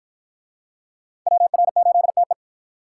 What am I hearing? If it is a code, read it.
Morse code, heard as a keyed tone, about 53 wpm, 700 Hz. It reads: WR8TE